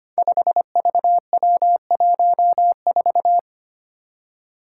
Morse code, heard as a keyed tone, about 25 wpm, 700 Hz. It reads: 5VW14
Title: Five Challenging Letter-Number Combinations 25wpm